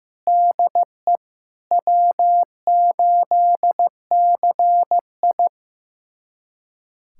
Morse code, heard as a keyed tone, about 15 wpm, 700 Hz. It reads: DE W8CI
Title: USA Callsign DE Callsign - 4 Characters - 15wpm